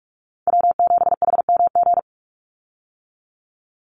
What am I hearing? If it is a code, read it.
Morse code, heard as a keyed tone, about 44 wpm, 700 Hz. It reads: W7HMG
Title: USA Call Signs 45wpm